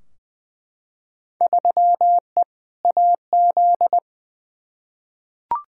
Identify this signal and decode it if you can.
Morse code, heard as a keyed tone, about 21 wpm, 700 Hz. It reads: 3E AZ
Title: Field Day Exchanges - 20wpm